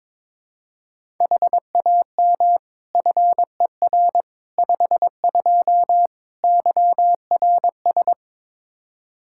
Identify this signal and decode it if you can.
Morse code, heard as a keyed tone, about 21 wpm, 700 Hz. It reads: HAM FER 52 YRS
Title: Practice QSO - Exchange 3 - More Info - 22wpm